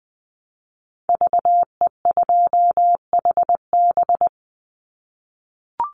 Morse code, heard as a keyed tone, about 20 wpm, 700 Hz. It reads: VE2HB
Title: Call Signs - 20wpm